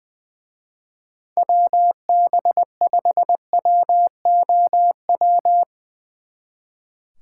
Morse code, heard as a keyed tone, about 21 wpm, 700 Hz. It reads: WB5WOW
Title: Speed Racing - USA 2x3 Call Signs 20wpm